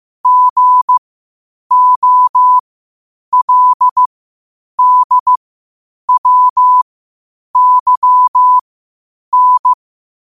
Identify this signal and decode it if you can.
Morse code, heard as a keyed tone, about 15 wpm, spaced as for 8 wpm, 1000 Hz. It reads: GOLDWYN
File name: qotdfiles / 8wpm